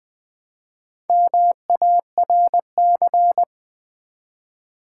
Morse code, heard as a keyed tone, about 20 wpm, 700 Hz. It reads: MARC
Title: Build A Word - 1 Letter At A Time - 5-Letter Words - 20wpm